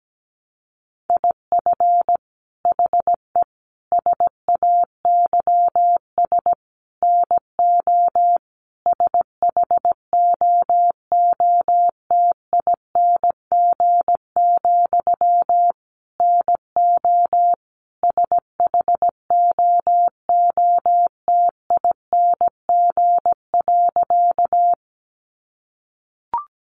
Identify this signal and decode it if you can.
Morse code, heard as a keyed tone, about 17 wpm, 700 Hz. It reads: IF HE SAYS NO SHOOTING, NO SHOOTING.
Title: A Sound Of Thunder 17wpm